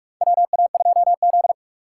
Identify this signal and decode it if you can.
Morse code, heard as a keyed tone, about 46 wpm, 700 Hz. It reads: WA2Z
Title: USA Call Signs - 4 Characters - 2x Character Spacing - 45wpm